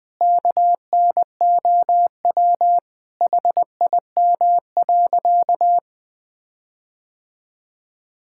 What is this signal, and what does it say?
Morse code, heard as a keyed tone, about 20 wpm, 700 Hz. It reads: KNOW HIM.
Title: Sentences with 4-Word Phrases - 20wpm